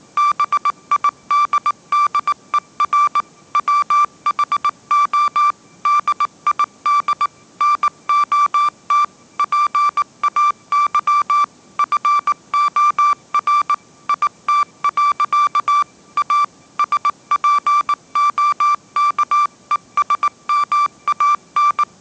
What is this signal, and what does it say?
Morse code, heard as a keyed tone, about 19 wpm, 1170 Hz. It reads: BIDDER WHO DID NOT PAY FOR IT. A SPOKESMAN